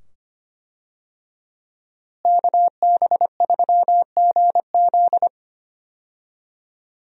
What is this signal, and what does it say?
Morse code, heard as a keyed tone, about 25 wpm, 700 Hz. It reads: KB3GZ
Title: USA Call Signs - 5 Characters - Repeated 2x - 25wpm